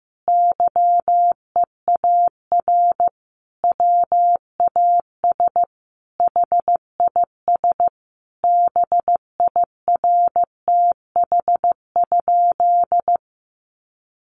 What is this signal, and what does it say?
Morse code, heard as a keyed tone, about 15 wpm, 700 Hz. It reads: YEAR WAS HIS BIRTH?
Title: Sentences from Top 800 Words 15wpm